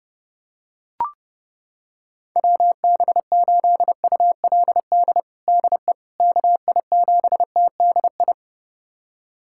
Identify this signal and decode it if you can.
Morse code, heard as a keyed tone, about 30 wpm, 700 Hz. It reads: WB8ULD DE KI7TDI
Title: USA Callsign DE Callsign - 6 Characters - 30wpm